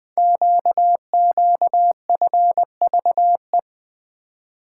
Morse code, heard as a keyed tone, about 20 wpm, 700 Hz. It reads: QQFVE
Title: Code Groups - 5 Letter:Numbers - 20wpm